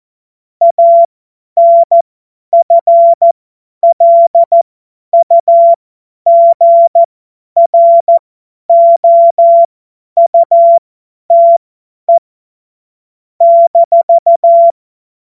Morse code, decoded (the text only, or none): ANFLUGROUTE -